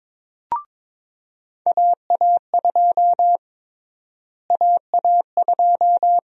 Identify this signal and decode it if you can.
Morse code, heard as a keyed tone, about 22 wpm, 700 Hz. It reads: AA2 AA2
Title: USA Call Sign Prefixes - Repeated 2x - 22wpm